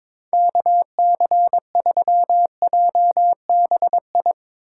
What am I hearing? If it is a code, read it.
Morse code, heard as a keyed tone, about 22 wpm, 700 Hz. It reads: KC3JBI
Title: USA Call Signs - 4x Character Spacing - 22wpm